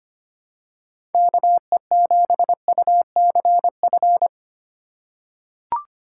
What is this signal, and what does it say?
Morse code, heard as a keyed tone, about 25 wpm, 700 Hz. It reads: KE7UCF
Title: USA Call Signs - 6x Character Spacing - 25wpm